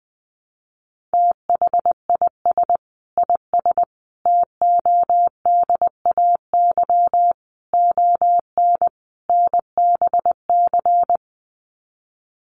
Morse code, heard as a keyed tone, about 20 wpm, 700 Hz. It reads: THIS IS TODAY ON NBC
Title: Sets of 5 Words Encore 20wpm